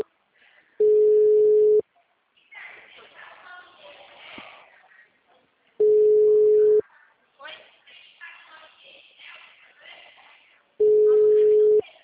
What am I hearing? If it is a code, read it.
Ringback tone (Europe, 425 Hz), with speech.